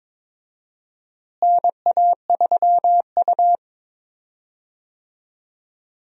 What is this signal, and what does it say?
Morse code, heard as a keyed tone, about 22 wpm, 700 Hz. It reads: NA3U
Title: USA Call Signs - 4 Characters - Repeated 3x - 22wpm